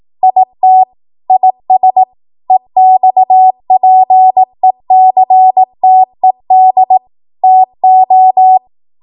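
Morse code, heard as a keyed tone, about 18 wpm, 750 Hz. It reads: IT IS EXPECTED TO
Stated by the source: 18 WPM Morse code practice from a national amateur radio association's code archive.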